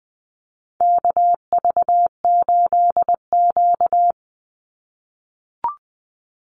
Morse code, heard as a keyed tone, about 20 wpm, 700 Hz. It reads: KV8Q
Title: Call Signs - 20wpm